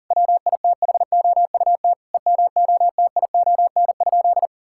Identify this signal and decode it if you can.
Morse code, heard as a keyed tone, about 40 wpm, 700 Hz. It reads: WITHOUT EMOTION?
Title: Sentences from Top 1600 Words 40wpm